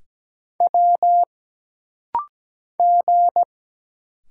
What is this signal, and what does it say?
Morse code, heard as a keyed tone, about 17 wpm, 700 Hz. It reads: W G
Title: Single Letter-Number - Rapid-Fire 17wpm